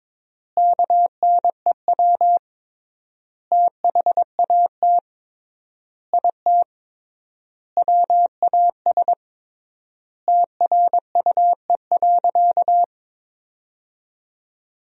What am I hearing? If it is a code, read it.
Morse code, heard as a keyed tone, about 22 wpm, 700 Hz. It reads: KNEW THAT IT WAS TRUE.
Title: 3x Word-Spacing - Sentences from Top 400 Words - 22wpm